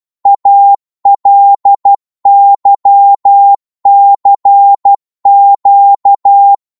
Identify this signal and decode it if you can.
Morse code, heard as a keyed tone, about 12 wpm, 800 Hz. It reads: ALYCQ